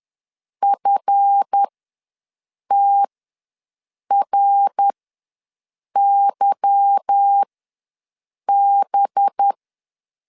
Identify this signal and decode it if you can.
Morse code, heard as a keyed tone, about 10 wpm, spaced as for 6 wpm, 800 Hz. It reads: FTRYB